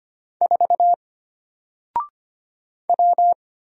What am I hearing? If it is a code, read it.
Morse code, heard as a keyed tone, about 25 wpm, 700 Hz. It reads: 4 W
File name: Single-Letter-Number-25wpm